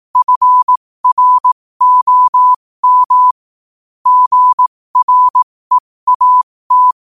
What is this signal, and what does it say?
Morse code, heard as a keyed tone, about 18 wpm, spaced as for 15 wpm, 1000 Hz. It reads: FROM GREAT
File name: qotdfiles / 15wpm